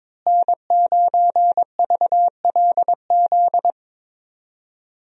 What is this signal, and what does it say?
Morse code, heard as a keyed tone, about 22 wpm, 700 Hz. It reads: N9VLZ
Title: USA Call Signs - 5 Characters - Repeated 2x - 22wpm